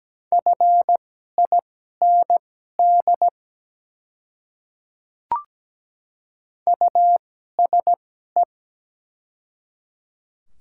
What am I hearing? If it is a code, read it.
Morse code, heard as a keyed tone, about 17 wpm, spaced as for 12 wpm, 700 Hz. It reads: FIND USE
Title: 2x Character Spacing - Top 200 Words - 17wpm